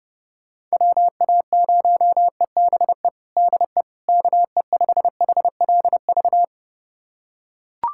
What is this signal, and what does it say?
Morse code, heard as a keyed tone, about 30 wpm, 700 Hz. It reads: WA0EBE DE KE5HLV
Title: USA Callsign DE Callsign - 6 Characters - 30wpm